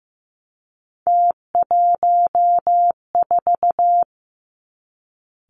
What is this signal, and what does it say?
Morse code, heard as a keyed tone, about 15 wpm, 700 Hz. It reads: T14
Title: Three Challenging Letter-Number Combinations 15wpm